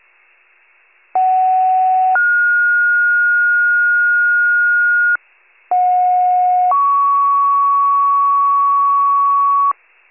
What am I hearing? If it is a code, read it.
Two-tone page, heard as a one-second tone, then a three-second tone, about 750 then 1430 Hz, about 730 then 1120 Hz, with noise. It is heard from 1.1 to 9.7 s.
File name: pager / 56 RESCUE / wcalert